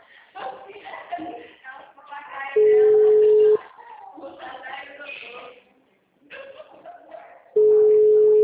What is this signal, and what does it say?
Ringback tone (Europe, 425 Hz), with speech.
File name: out- -2006- - - .45984